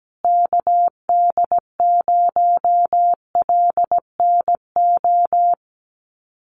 Morse code, heard as a keyed tone, about 17 wpm, 700 Hz. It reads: KD0LNO
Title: USA Call Signs Repeated 3x 17wpm